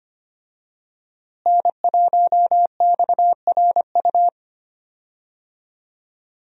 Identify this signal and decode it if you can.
Morse code, heard as a keyed tone, about 25 wpm, 700 Hz. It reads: N1XRU
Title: USA Call Signs - 5 Characters - Repeated 2x - 25wpm